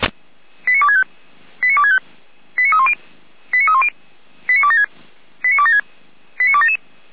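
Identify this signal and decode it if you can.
Five-tone selective call (ZVEI), heard as quick runs of five tones, tones of about 70 ms. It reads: 89265 89265 89310 89310 89276 89276 89277
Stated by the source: ZVEI five-tone selective call received by SDR, from feuerwehr-orlamuende.org (fire brigade Orlamünde).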